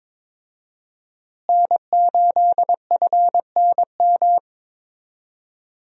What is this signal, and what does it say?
Morse code, heard as a keyed tone, about 22 wpm, 700 Hz. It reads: N8FNM
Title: USA Call Signs - 5 Characters - Repeated 3x - 22wpm